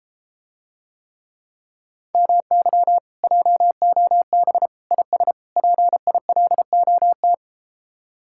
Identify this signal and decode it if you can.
Morse code, heard as a keyed tone, about 33 wpm, 700 Hz. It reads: MY JOB IS PILOT